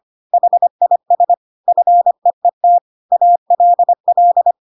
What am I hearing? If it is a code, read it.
Morse code, heard as a keyed tone, about 25 wpm, 700 Hz. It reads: HIS FEET ALL
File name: Pinocchio / Pinocchio ch6 25 25 ik1hge